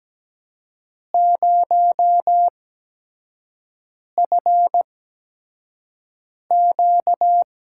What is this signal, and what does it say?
Morse code, heard as a keyed tone, about 17 wpm, spaced as for 5 wpm, 700 Hz. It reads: 0FQ
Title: USA Call Signs - 8x Character Spacing - 17wpm